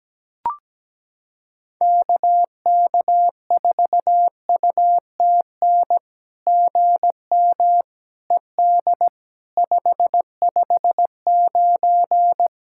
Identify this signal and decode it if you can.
Morse code, heard as a keyed tone, about 17 wpm, 700 Hz. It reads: KK4UTN GM ED 559
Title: SOTA QSO Practice 17wpm